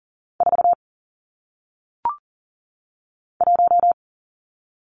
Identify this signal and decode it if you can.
Morse code, heard as a keyed tone, about 40 wpm, 700 Hz. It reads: 4 1
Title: Numbers - 1-Digit 40wpm